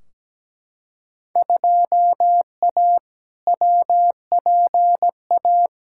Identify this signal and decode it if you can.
Morse code, heard as a keyed tone, about 17 wpm, 700 Hz. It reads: 2A WPA
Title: Field Day Exchanges - 17wpm